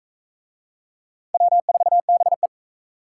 Morse code, heard as a keyed tone, about 38 wpm, 700 Hz. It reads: W4BE